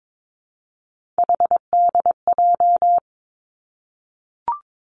Morse code, heard as a keyed tone, about 22 wpm, 700 Hz. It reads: HDJ
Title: Three Challenging Letter-Number Combinations 22wpm